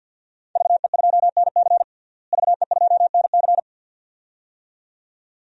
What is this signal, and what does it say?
Morse code, heard as a keyed tone, about 46 wpm, 700 Hz. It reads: VE2NC VE2NC
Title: DXCC-100 Call Signs - 5 Characters - Repeated 2x - 50wpm